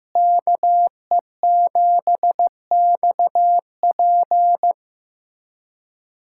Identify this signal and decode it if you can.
Morse code, heard as a keyed tone, about 15 wpm, 700 Hz. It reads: KE7XP
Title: USA Callsign DE Callsign 15wpm